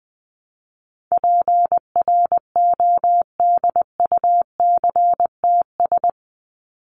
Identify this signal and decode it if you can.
Morse code, heard as a keyed tone, about 19 wpm, 700 Hz. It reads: PRODUCTS
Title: ES:S Suffix - Top 100 Words with Base Words 20wpm